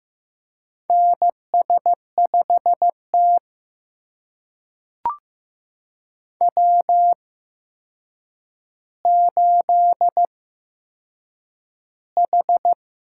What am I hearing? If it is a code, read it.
Morse code, heard as a keyed tone, about 15 wpm, spaced as for 7 wpm, 700 Hz. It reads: NS5T W8H
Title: USA Call Signs - 4 Characters - 8x Character Spacing - 15wpm